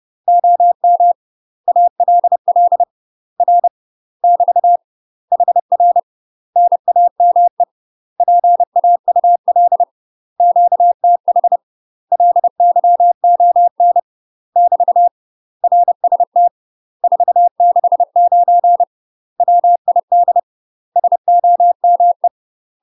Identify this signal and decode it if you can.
Morse code, heard as a keyed tone, about 30 wpm, 700 Hz. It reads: OM ALL R <BT> HR NAME PAUL QTH LYON <BT> RST 469 WID SOME
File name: UK MASTER QSOs TRAINING / 30WPM / UK MASTER QSOs 0001